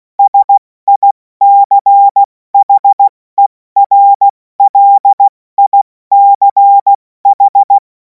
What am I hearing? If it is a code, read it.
Morse code, heard as a keyed tone, about 16 wpm, 800 Hz. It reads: SICHERLICH
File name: Morsekurs / Witze / Witz 02. Juni 2026 70 F80